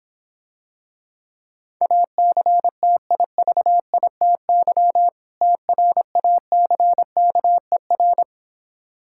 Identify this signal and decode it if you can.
Morse code, heard as a keyed tone, about 26 wpm, 690 Hz. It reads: ACTIVITY TRACKER